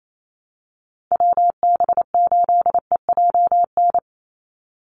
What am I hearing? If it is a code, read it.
Morse code, heard as a keyed tone, about 28 wpm, 700 Hz. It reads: WB8EJN
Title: Call Signs Repeated 3x - 28wpm